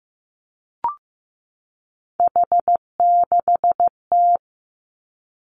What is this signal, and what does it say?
Morse code, heard as a keyed tone, about 15 wpm, 700 Hz. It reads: H6T
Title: Three Challenging Letter-Number Combinations 15wpm